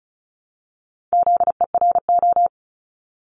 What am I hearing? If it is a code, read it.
Morse code, heard as a keyed tone, about 36 wpm, 700 Hz. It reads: ZERO
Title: Multi-letter Phonemes Repeated 3x 35wpm